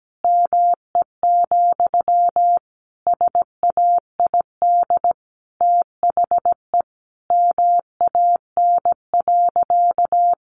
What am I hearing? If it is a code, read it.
Morse code, heard as a keyed tone, about 17 wpm, 700 Hz. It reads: ME, SAID THE MAN.
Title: Sentences from Top 200 Words 17wpm